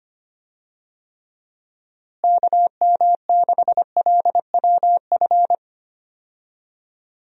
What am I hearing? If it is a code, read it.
Morse code, heard as a keyed tone, about 25 wpm, 700 Hz. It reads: KM6LWF
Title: USA Call Signs - 6 Characters - Repeated 2x - 25wpm